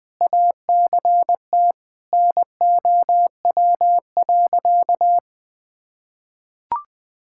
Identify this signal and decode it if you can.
Morse code, heard as a keyed tone, about 21 wpm, 700 Hz. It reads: ACT NOW.
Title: Sentences with 8-Word Phrases - 20wpm